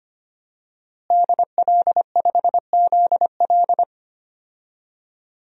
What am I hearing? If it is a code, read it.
Morse code, heard as a keyed tone, about 25 wpm, 700 Hz. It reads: DL5ZL
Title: DXCC-100 Call Signs - 5 Characters - Repeated 3x - 25wpm